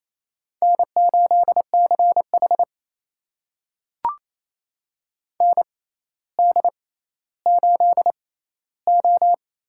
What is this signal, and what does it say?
Morse code, heard as a keyed tone, about 28 wpm, spaced as for 14 wpm, 700 Hz. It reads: N8CH ND8O
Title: USA Call Signs - 4 Characters - 6x Character Spacing - 28wpm